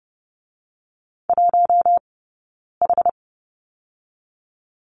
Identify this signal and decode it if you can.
Morse code, heard as a keyed tone, about 30 wpm, 700 Hz. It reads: 1 H
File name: Lesson 20d - Number 3 - Two Characters 30wpm